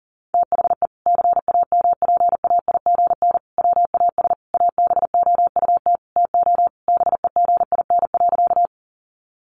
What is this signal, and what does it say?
Morse code, heard as a keyed tone, about 40 wpm, 700 Hz. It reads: THE CAMPAIGN WAS ABOUT TO BEGIN.
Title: Sentences from Top 1200 Words 40wpm